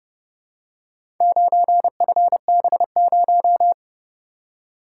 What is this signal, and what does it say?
Morse code, heard as a keyed tone, about 31 wpm, 700 Hz. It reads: 9FB0